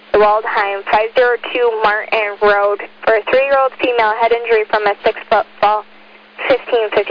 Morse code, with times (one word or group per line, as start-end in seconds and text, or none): none